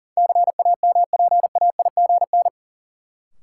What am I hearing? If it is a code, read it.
Morse code, heard as a keyed tone, about 38 wpm, 700 Hz. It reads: CAMPAIGN
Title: Sentences from Top 1200 Words - New Word Drill - 40wpm